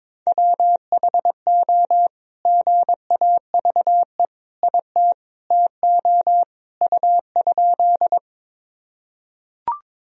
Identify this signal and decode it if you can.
Morse code, heard as a keyed tone, about 22 wpm, 700 Hz. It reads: WHO GAVE IT TO U?
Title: Sentences from Top 500 Words and Abbreviations 22wpm